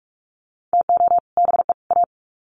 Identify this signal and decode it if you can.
Morse code, heard as a keyed tone, about 44 wpm, 700 Hz. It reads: TO BE A
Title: Sets of 3 Words 45wpm